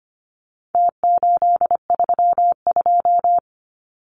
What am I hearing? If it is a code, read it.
Morse code, heard as a keyed tone, about 25 wpm, 700 Hz. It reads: T832
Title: Numbers - Contest Style - 4-Digit 25wpm